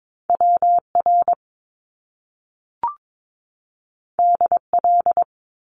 Morse code, heard as a keyed tone, about 22 wpm, 700 Hz. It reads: WR DL
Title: Top Two Letter Combinations in Words 22wpm